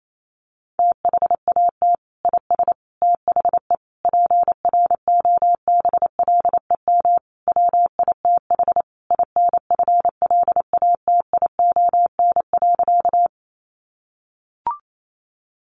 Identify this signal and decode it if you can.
Morse code, heard as a keyed tone, about 28 wpm, 700 Hz. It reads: THAT IS THE PROBLEM WITH INFLATION.
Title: Sentences from Top 1100 Words 28wpm